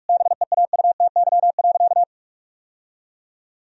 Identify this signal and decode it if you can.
Morse code, heard as a keyed tone, about 46 wpm, 700 Hz. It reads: BEAUTY.
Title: Sentences from Top 1000 Words - Review Part 4 - 45wpm